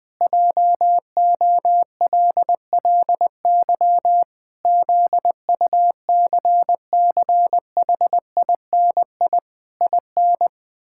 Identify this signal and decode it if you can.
Morse code, heard as a keyed tone, about 20 wpm, 700 Hz. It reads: JOLLY ZUCCHINI IN
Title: Pangram Challenge v2 - 20wpm